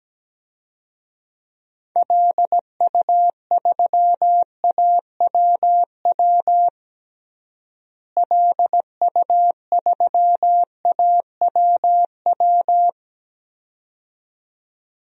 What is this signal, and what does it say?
Morse code, heard as a keyed tone, about 17 wpm, 700 Hz. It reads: LU3AWW LU3AWW
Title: DXCC-100 Call Signs - 6 Characters - Repeated 2x - 17wpm